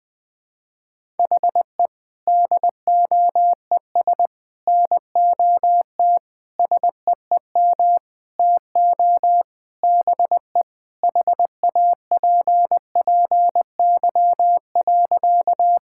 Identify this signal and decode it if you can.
Morse code, heard as a keyed tone, about 20 wpm, 700 Hz. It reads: HE DOES NOT SEEM TO BE HAPPY.
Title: Sentences with 5-Word Phrases - 20wpm